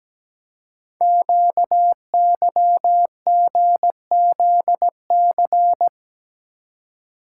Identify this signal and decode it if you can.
Morse code, heard as a keyed tone, about 17 wpm, 700 Hz. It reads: QYGZC
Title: Code Groups - 5 Letters - 17wpm